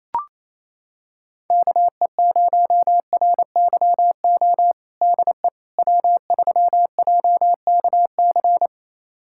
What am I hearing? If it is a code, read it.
Morse code, heard as a keyed tone, about 28 wpm, 700 Hz. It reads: KE0RYO DE W3JKC
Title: USA Callsign DE Callsign 28wpm